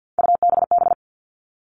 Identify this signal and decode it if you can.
Morse code, heard as a keyed tone, about 48 wpm, 700 Hz. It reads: UBB
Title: All Cut Numbers - 3-Digit 50wpm